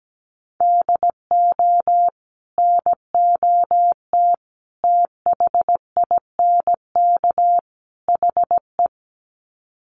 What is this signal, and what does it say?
Morse code, heard as a keyed tone, about 17 wpm, 700 Hz. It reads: DO NOT THINK HE
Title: Sets of 4 Words 17wpm